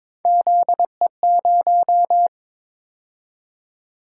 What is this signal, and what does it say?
Morse code, heard as a keyed tone, about 22 wpm, 700 Hz. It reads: ZE0
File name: practice / 22wpm / Three Letter-Number Combinations - 22wpm